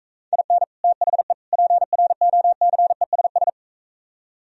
Morse code, heard as a keyed tone, about 45 wpm, 700 Hz. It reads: IN THE PROCESS